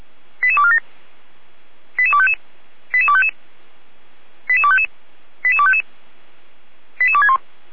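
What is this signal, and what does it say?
Five-tone selective call (ZVEI), heard as quick runs of five tones, tones of about 70 ms. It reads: 88247 88255 88255 88255 88255 88261
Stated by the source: ZVEI five-tone selective call received by SDR on a server of the Feuerwehr Orlamünde fire brigade.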